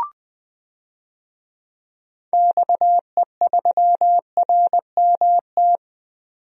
Morse code, heard as a keyed tone, about 21 wpm, 700 Hz. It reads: XE3RMT
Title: DXCC-100 Call Signs - 6 Characters - 20wpm